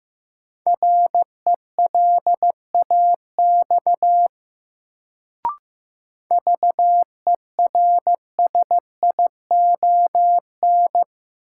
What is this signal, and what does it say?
Morse code, heard as a keyed tone, about 15 wpm, 700 Hz. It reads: RELAX VERSION
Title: Sentences from Top 1400 Words - New Word Drill Alt - 15wpm